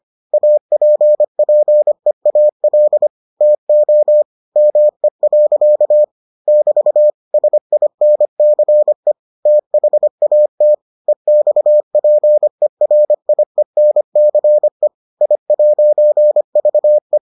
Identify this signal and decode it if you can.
Morse code, heard as a keyed tone, about 25 wpm, 600 Hz. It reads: APPEAL TO ME. <BT> SINCE THAT EXPERIENCE I'VE